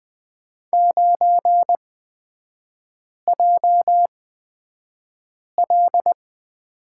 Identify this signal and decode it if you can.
Morse code, heard as a keyed tone, about 20 wpm, spaced as for 5 wpm, 700 Hz. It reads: 9JL
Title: 243 Lesson 39e - Letter z - Three Characters 20wpm